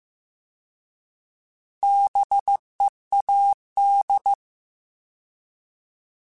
Morse code, heard as a keyed tone, about 15 wpm, 800 Hz. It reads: BEAD
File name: Alphabetical / Alphabetical mp3s / Bs @13